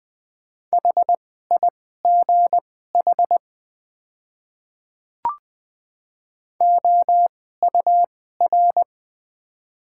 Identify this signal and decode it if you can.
Morse code, heard as a keyed tone, about 20 wpm, spaced as for 14 wpm, 700 Hz. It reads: HIGH OUR
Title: 2x Character Spacing - Top 500 Words - 20wpm